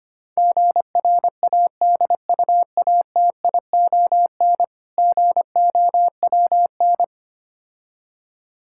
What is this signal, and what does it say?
Morse code, heard as a keyed tone, about 25 wpm, 700 Hz. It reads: GRADUATION GOWN
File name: practice / 25wpm / Difficult Words - Speed-Racing with Hints - 25wpm